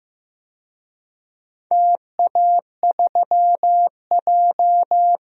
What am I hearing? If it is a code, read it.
Morse code, heard as a keyed tone, about 15 wpm, 700 Hz. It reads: TA3J
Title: DXCC-100 Call Signs - 4 Characters - Repeated 3x - 15wpm